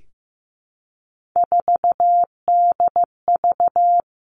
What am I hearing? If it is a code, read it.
Morse code, heard as a keyed tone, about 15 wpm, 700 Hz. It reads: 4DV
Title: Three Challenging Letter-Number Combinations 15wpm